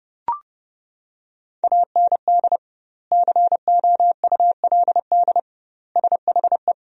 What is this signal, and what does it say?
Morse code, heard as a keyed tone, about 32 wpm, 700 Hz. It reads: AND COULD SHE
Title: 2x Word-Spacing - Sentences from Top 100 Words - 30wpm